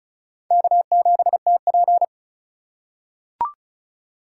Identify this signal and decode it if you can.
Morse code, heard as a keyed tone, about 36 wpm, 700 Hz. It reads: K7TP
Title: USA Call Signs - 4 Characters - 2x Character Spacing - 35wpm